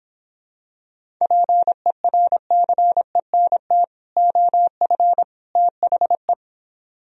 Morse code, heard as a keyed tone, about 26 wpm, 700 Hz. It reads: PERCENT OF THE